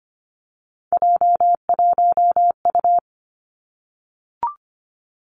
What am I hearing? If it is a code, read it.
Morse code, heard as a keyed tone, about 25 wpm, 700 Hz. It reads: J1U
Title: Three Challenging Letter-Number Combinations 25wpm